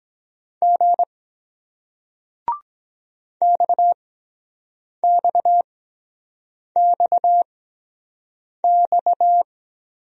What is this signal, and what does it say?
Morse code, heard as a keyed tone, about 23 wpm, spaced as for 7 wpm, 700 Hz. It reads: G XXXX